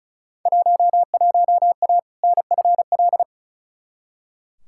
Morse code, heard as a keyed tone, about 36 wpm, 700 Hz. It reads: 11A NFL
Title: Field Day Exchanges Encore - 35wpm